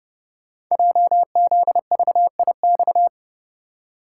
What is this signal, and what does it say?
Morse code, heard as a keyed tone, about 29 wpm, 700 Hz. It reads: JZVIX
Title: Code Groups - 5 Letters - 30wpm